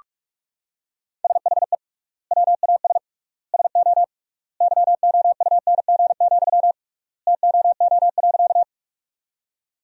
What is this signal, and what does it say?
Morse code, heard as a keyed tone, about 48 wpm, 700 Hz. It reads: SHE WAS SO YOUNG, TOO.
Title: 3x Word-Spacing - Sentences from Top 400 Words - 45wpm